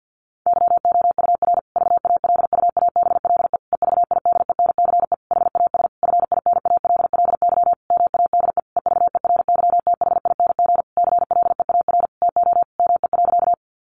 Morse code, heard as a keyed tone, about 50 wpm, 700 Hz. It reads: YOUR VALUABLE EVIDENCE HAS FINALLY MADE EVERYTHING CLEAR TO ME.
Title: Sentences from Top 1300 Words 50wpm